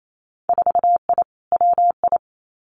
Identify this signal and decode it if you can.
Morse code, heard as a keyed tone, about 28 wpm, 700 Hz. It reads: 4I WI
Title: Winter Field Day Exchanges 28wpm